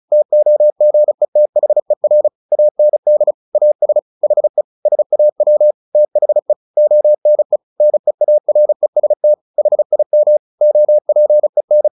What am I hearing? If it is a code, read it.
Morse code, heard as a keyed tone, about 34 wpm, 600 Hz. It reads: TOGETHER AND AS HE SAW THE ONE NEAREST HIM OPEN
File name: the old man and the sea 2 35wpm